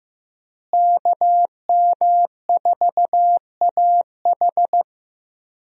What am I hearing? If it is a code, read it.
Morse code, heard as a keyed tone, about 15 wpm, 700 Hz. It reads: KM4AH
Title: USA Call Signs - 5 Characters - Repeated 2x - 15wpm